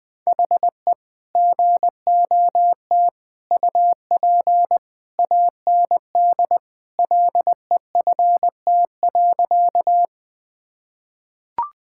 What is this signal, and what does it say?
Morse code, heard as a keyed tone, about 20 wpm, 700 Hz. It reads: HE GOT UP AND LEFT.
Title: Sentences with 2-Word Phrases - 20wpm